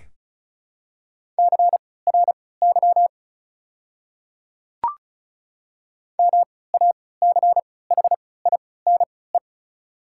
Morse code, heard as a keyed tone, about 36 wpm, spaced as for 20 wpm, 700 Hz. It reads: CRY MACHINE
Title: 3x Character Spacing - Top 500 Words - 35wpm